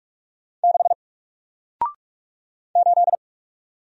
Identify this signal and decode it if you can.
Morse code, heard as a keyed tone, about 46 wpm, 700 Hz. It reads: 6 8
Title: Single Letter-Number Alt 45wpm